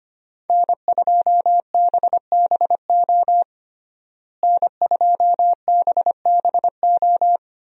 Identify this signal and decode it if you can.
Morse code, heard as a keyed tone, about 25 wpm, 700 Hz. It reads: N2BBO N2BBO
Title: USA Call Signs - 5 Characters - Repeated 2x - 25wpm